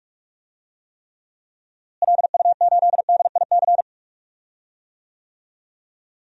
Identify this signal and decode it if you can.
Morse code, heard as a keyed tone, about 46 wpm, 700 Hz. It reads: LU8DIC
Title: DXCC-100 Call Signs - 6 Characters - Repeated 2x - 45wpm